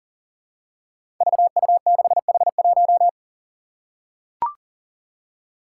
Morse code, heard as a keyed tone, about 40 wpm, 700 Hz. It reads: VU6H1
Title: Five Challenging Letter-Number Combinations 40wpm